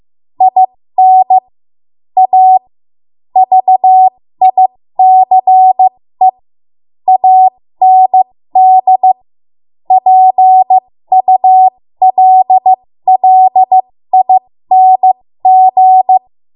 Morse code, heard as a keyed tone, about 15 wpm, spaced as for 13 wpm, 750 Hz. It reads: IN A VICE AND PULLING
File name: Morse / 13WPM